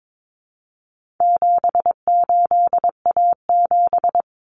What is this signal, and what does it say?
Morse code, heard as a keyed tone, about 22 wpm, 700 Hz. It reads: 78A7
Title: Common Cut Numbers - 4-Digit 22wpm